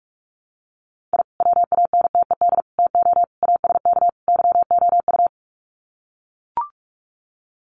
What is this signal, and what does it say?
Morse code, heard as a keyed tone, about 44 wpm, 700 Hz. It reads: I WANTED TO ASK YOU
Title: Sets of 5 Words Encore2 45wpm